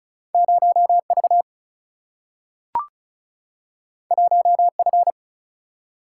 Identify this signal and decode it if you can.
Morse code, heard as a keyed tone, about 36 wpm, 700 Hz. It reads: 0V 1F
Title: Single-Letter Single-Number 35wpm